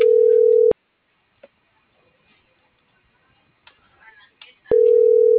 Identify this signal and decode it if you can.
Ringback tone (North America), 440 + 480 Hz.